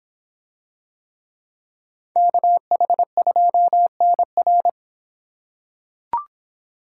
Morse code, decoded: KH2NR

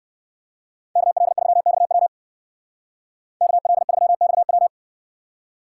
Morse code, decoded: KL3XW KL3XW

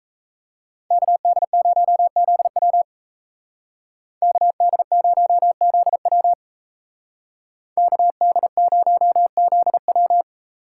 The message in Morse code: KD0ZW KD0ZW KD0ZW

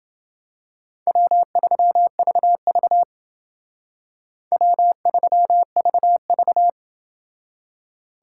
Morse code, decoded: W3VV W3VV

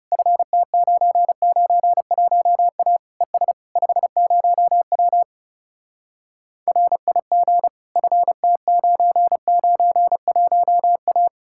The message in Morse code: FT991A ES 50W RIG FT991A

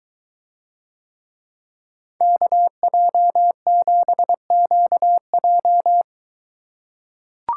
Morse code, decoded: KJ7QJ

